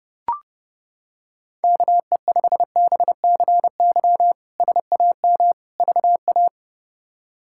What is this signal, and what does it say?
Morse code, heard as a keyed tone, about 30 wpm, 700 Hz. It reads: KE5BCY SAM VA
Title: K1USN SST QSO Practice - 30wpm